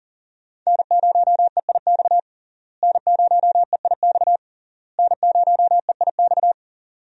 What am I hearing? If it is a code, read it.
Morse code, heard as a keyed tone, about 40 wpm, 700 Hz. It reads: N0EIX N0EIX N0EIX
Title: USA Call Signs - 5 Characters - Repeated 3x - 40wpm